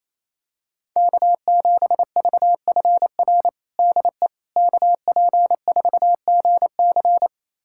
Morse code, heard as a keyed tone, about 28 wpm, 700 Hz. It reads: K7VFR DE KP4GC
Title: USA Callsign DE Callsign - 5 Characters - 28wpm